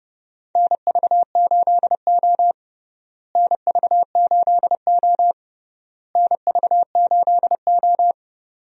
Morse code, heard as a keyed tone, about 30 wpm, 700 Hz. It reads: NV8O NV8O NV8O
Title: USA Call Signs - 4 Characters - Repeated 3x - 30wpm